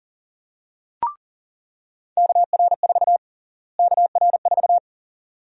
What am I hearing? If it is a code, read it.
Morse code, heard as a keyed tone, about 40 wpm, 700 Hz. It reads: KR4 KR4
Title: USA Call Sign Prefixes - Repeated 2x - 40wpm